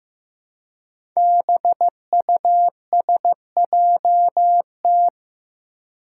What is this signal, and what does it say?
Morse code, heard as a keyed tone, about 15 wpm, 700 Hz. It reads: BUSJT
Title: Five Challenging Letter-Number Combinations 15wpm